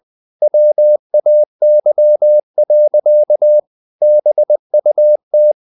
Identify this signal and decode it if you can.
Morse code, heard as a keyed tone, about 20 wpm, 600 Hz. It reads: WAY. BUT
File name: the old man and the sea / the old man and the sea part 1 20wpm